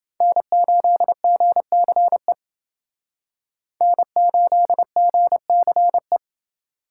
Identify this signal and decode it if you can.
Morse code, heard as a keyed tone, about 29 wpm, 700 Hz. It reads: N8GCE N8GCE